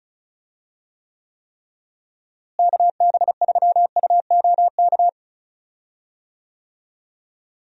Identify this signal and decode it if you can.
Morse code, heard as a keyed tone, about 34 wpm, 700 Hz. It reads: KB3UOK
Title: USA Call Signs - 6 Characters - Repeated 3x - 35wpm